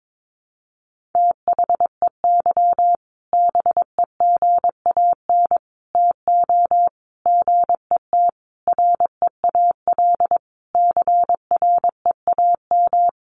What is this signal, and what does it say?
Morse code, heard as a keyed tone, about 22 wpm, 700 Hz. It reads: THEY BEGAN TO GET REAL CREAM
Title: Sentences from Top 1500 Words 22wpm